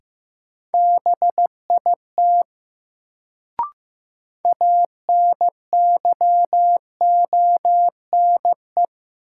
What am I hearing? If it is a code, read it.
Morse code, heard as a keyed tone, about 15 wpm, 700 Hz. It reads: BIT ANYONE
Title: Sentences from Top 900 Words - New Word Drill - 15wpm